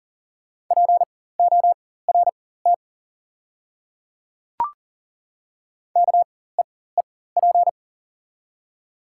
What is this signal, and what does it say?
Morse code, heard as a keyed tone, about 38 wpm, spaced as for 18 wpm, 690 Hz. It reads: PORT KEEP
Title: 4x Character Spacing - Top 500 Words - 40wpm